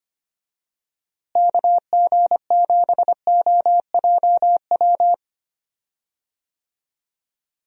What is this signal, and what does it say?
Morse code, heard as a keyed tone, about 25 wpm, 700 Hz. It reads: KG7OJW
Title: USA Call Signs - 6 Characters - Repeated 2x - 25wpm